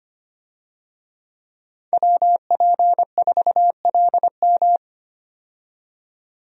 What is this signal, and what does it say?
Morse code, heard as a keyed tone, about 25 wpm, 690 Hz. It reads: WP4LM
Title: USA Call Signs - 5 Characters - 25wpm